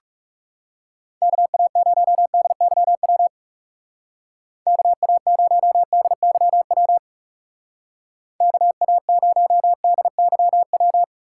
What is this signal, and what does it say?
Morse code, heard as a keyed tone, about 36 wpm, 700 Hz. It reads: KA0DYW KA0DYW KA0DYW
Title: Speed Racing - USA Call Signs 30wpm